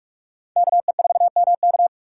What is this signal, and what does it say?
Morse code, heard as a keyed tone, about 56 wpm, 700 Hz. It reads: KE4MK
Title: Speed Racing - USA 2x2 Call Signs 45wpm